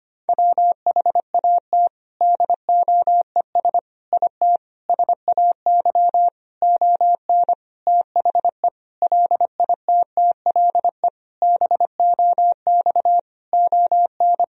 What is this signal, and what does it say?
Morse code, heard as a keyed tone, about 25 wpm, 700 Hz. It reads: WHAT DOES IT SAY ON THE LITTLE BOX ON
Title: Sentences from Top 1000 Words - Review Part 2 - 25wpm